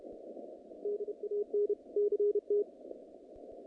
Morse code, heard as a keyed tone, about 31 wpm, 410 Hz. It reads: DAN CT
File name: contests / naqp-cw-k5tr-op-k5pi- -2200